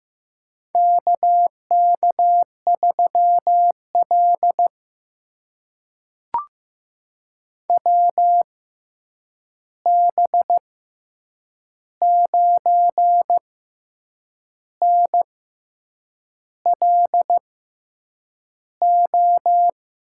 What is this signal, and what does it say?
Morse code, heard as a keyed tone, about 15 wpm, spaced as for 7 wpm, 700 Hz. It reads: KK3L WB9NLO
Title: USA Call Signs - 6x Character Spacing - 15wpm